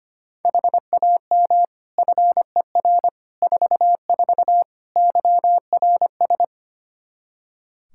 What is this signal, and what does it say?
Morse code, heard as a keyed tone, about 25 wpm, 700 Hz. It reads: HAM FER 44 YRS
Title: K1USN SST QSO Practice 25wpm